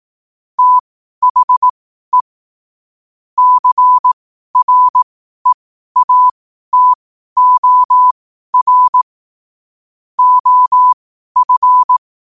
Morse code, heard as a keyed tone, about 18 wpm, spaced as for 12 wpm, 1000 Hz. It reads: THE CREATOR OF
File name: qotdfiles / 12wpm